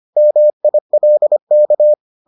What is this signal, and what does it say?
Morse code, heard as a keyed tone, about 25 wpm, 600 Hz. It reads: MILK